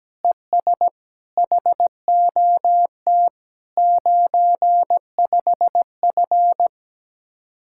Morse code, heard as a keyed tone, about 17 wpm, 700 Hz. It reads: ES HOT 95F
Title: Speed-Racing Common QSO Phrases - 17wpm